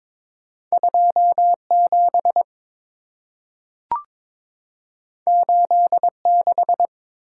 Morse code, heard as a keyed tone, about 22 wpm, 700 Hz. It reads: 27 86
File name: practice / 22wpm / Two Letter-Number Combinations - 22wpm